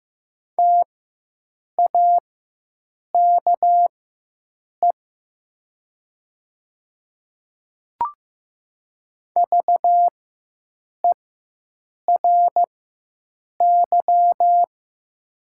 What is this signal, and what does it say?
Morse code, heard as a keyed tone, about 15 wpm, spaced as for 7 wpm, 690 Hz. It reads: TAKE VERY
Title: 4x Character Spacing - Top 200 Words - 15wpm